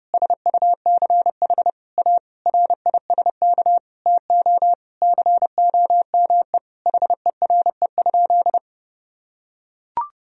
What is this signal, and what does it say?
Morse code, heard as a keyed tone, about 32 wpm, 700 Hz. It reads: SUCH A RISK TO COME HERE?
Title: Sentences from Top 1000 Words - Review Part 4 - 30wpm